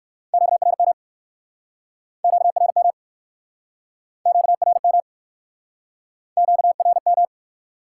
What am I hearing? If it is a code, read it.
Morse code, heard as a keyed tone, about 61 wpm, 700 Hz. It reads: QRM QRM QRM QRM